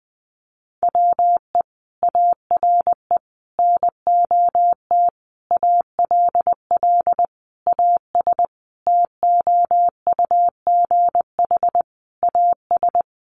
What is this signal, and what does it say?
Morse code, heard as a keyed tone, about 19 wpm, 700 Hz. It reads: WE ARE NOT ALL AS TOUGH AS
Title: Sentences from Top 1200 Words 20wpm